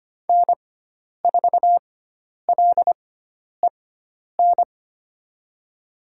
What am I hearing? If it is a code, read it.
Morse code, heard as a keyed tone, about 25 wpm, spaced as for 10 wpm, 700 Hz. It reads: N4LEN
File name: Lesson 16g - Letter c - Call Signs 25wpm